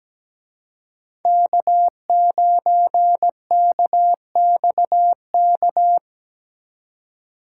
Morse code, heard as a keyed tone, about 20 wpm, 700 Hz. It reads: K9KXK
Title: Speed Racing - USA 1x3 Call Signs 17wpm